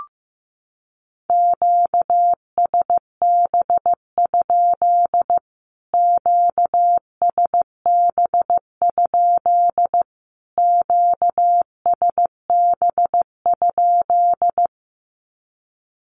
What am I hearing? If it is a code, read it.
Morse code, heard as a keyed tone, about 15 wpm, 700 Hz. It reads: QSB? QSB? QSB?
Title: Q-Codes Repeated 3x 15wpm